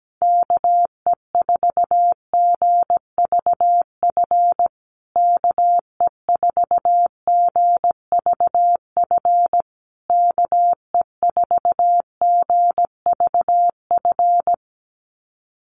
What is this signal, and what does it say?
Morse code, heard as a keyed tone, about 17 wpm, 700 Hz. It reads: KE4GVF KE4GVF KE4GVF
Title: USA Call Signs Repeated 3x 17wpm